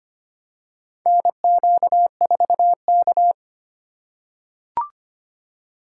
Morse code, heard as a keyed tone, about 25 wpm, 700 Hz. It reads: NQ4K
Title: USA Call Signs - 4 Characters - 4x Character Spacing - 25wpm